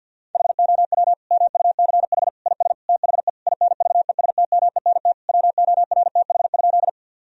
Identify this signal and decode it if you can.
Morse code, heard as a keyed tone, about 51 wpm, 700 Hz. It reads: HOW MUCH IS THE INVESTMENT WORTH?
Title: Sentences from Top 1000 Words - Review Part 4 - 50wpm